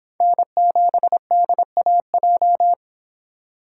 Morse code, heard as a keyed tone, about 26 wpm, 700 Hz. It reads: N7DAJ